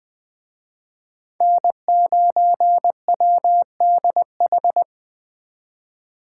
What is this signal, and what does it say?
Morse code, heard as a keyed tone, about 21 wpm, 700 Hz. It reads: N9WDH